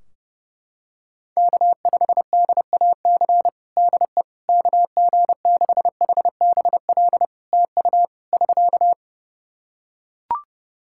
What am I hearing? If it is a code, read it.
Morse code, heard as a keyed tone, about 30 wpm, 700 Hz. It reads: K5DAC DE KG6HBL TU <SK>
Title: K1USN SST QSO Practice 30wpm